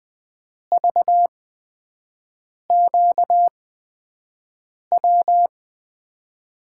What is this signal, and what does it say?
Morse code, heard as a keyed tone, about 20 wpm, spaced as for 5 wpm, 700 Hz. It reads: VQW
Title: USA Call Signs - 8x Character Spacing - 20wpm